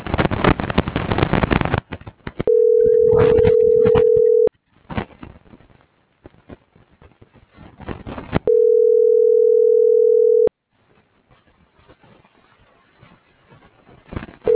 Ringback tone (North America), 440 + 480 Hz.